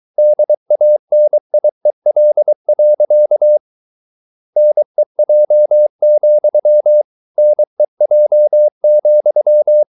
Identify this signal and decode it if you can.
Morse code, heard as a keyed tone, about 23 wpm, 600 Hz. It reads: DANIEL. NEJ, NEJ,
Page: Ferie 23 wpm